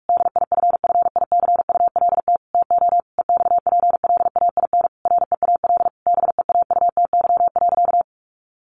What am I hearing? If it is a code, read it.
Morse code, heard as a keyed tone, about 44 wpm, 700 Hz. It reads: DIFFICULT TO EXPLAIN REAL BEAUTY.
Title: Sentences from Top 1100 Words 45wpm